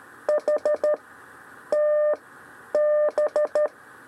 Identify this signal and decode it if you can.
Morse code, heard as a keyed tone, about 13 wpm, 600 Hz. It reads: H T B
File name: 6 lezione cw 1